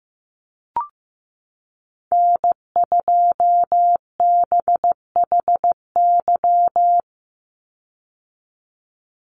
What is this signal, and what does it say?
Morse code, heard as a keyed tone, about 15 wpm, 700 Hz. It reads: N2BHY
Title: USA Call Signs 15wpm